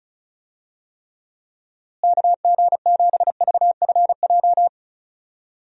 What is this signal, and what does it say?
Morse code, heard as a keyed tone, about 36 wpm, 700 Hz. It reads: KG7VFJ
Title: USA Call Signs - 6 Characters - Repeated 3x - 35wpm